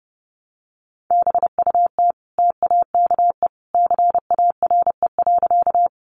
Morse code, heard as a keyed tone, about 30 wpm, 700 Hz. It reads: BUT TAKE CARE.
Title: Sentences from Top 300 Words 30wpm